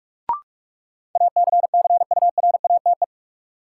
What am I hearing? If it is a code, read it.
Morse code, heard as a keyed tone, about 46 wpm, 700 Hz. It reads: ACCURATE
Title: Sentences from Top 1200 Words - New Word Drill - 45wpm